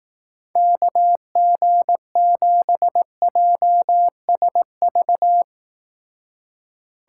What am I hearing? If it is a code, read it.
Morse code, heard as a keyed tone, about 18 wpm, 700 Hz. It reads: KG7JSV
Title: Speed Racing - USA 2x3 Call Signs 15wpm